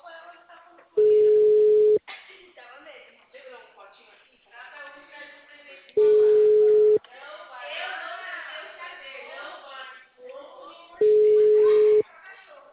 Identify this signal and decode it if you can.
Ringback tone (Europe, 425 Hz), with speech.